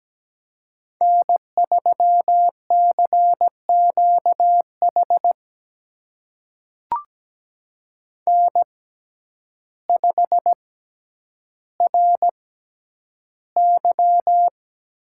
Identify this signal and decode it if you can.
Morse code, heard as a keyed tone, about 17 wpm, spaced as for 9 wpm, 700 Hz. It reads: N3CQH N5RY
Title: USA Call Signs - 5 Characters - 6x Character Spacing - 17wpm